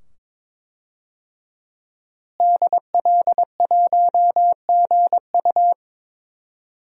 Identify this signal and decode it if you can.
Morse code, heard as a keyed tone, about 21 wpm, 700 Hz. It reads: DL1GU
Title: DXCC-100 Call Signs - 5 Characters - Repeated 2x - 22wpm